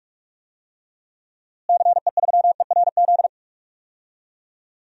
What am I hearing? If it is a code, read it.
Morse code, heard as a keyed tone, about 46 wpm, 700 Hz. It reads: CE3ERZ
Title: DXCC-100 Call Signs - 6 Characters - Repeated 3x - 45wpm